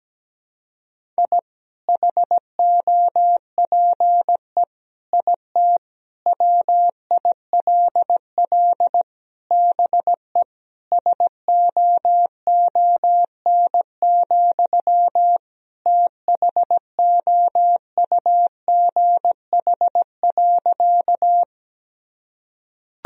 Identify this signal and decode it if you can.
Morse code, heard as a keyed tone, about 17 wpm, 700 Hz. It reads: I HOPE IT WILL BE SOON, THOUGH.
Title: Sentences from Top 1000 Words - Review Part 2 - 17wpm